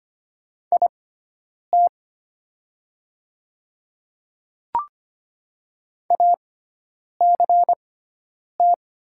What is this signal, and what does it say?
Morse code, heard as a keyed tone, about 25 wpm, spaced as for 9 wpm, 700 Hz. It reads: IT ACT
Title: 6x Character Spacing - Top 200 Words - 25wpm